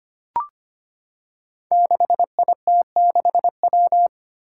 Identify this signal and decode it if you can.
Morse code, heard as a keyed tone, about 25 wpm, 700 Hz. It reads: 6IT6W
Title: Five Challenging Letter-Number Combinations 25wpm